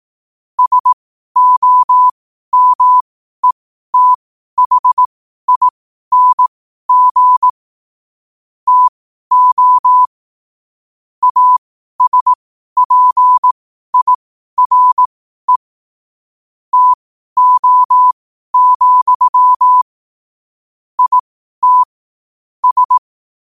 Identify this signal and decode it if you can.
Morse code, heard as a keyed tone, about 18 wpm, spaced as for 12 wpm, 1000 Hz. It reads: SOMETHING TO ASPIRE TO, IT S
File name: qotdfiles / 12wpm